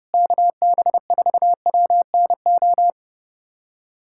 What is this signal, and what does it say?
Morse code, heard as a keyed tone, about 32 wpm, 700 Hz. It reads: KB4WNO
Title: USA Callsign DE Callsign 30wpm